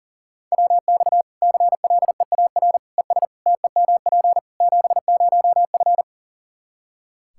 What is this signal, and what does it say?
Morse code, heard as a keyed tone, about 40 wpm, 700 Hz. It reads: WX CLEAR ES TEMP 70F
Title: Practice QSO - Exchange 2 - Info - 40wpm